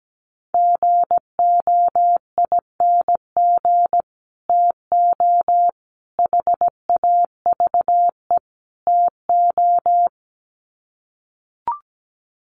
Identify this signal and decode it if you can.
Morse code, heard as a keyed tone, about 17 wpm, 700 Hz. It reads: GOING TO HAVE TO
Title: Sets of 4 Words 17wpm